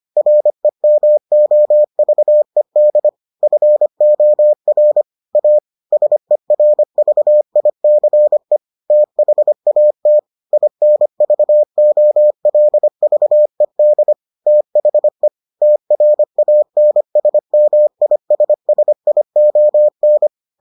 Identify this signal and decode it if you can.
Morse code, heard as a keyed tone, about 25 wpm, 600 Hz. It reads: REMOVED FOR A SERVICE THAT INVOLVED THE TRANSMISSION